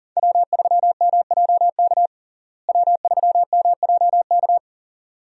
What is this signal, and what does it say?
Morse code, heard as a keyed tone, about 40 wpm, 700 Hz. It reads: W3MJK W3MJK
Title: USA Call Signs - 5 Characters - Repeated 2x - 40wpm